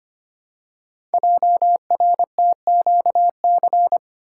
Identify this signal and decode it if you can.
Morse code, heard as a keyed tone, about 25 wpm, 700 Hz. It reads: JRTQC
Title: Code Groups - 5 Letters - 25wpm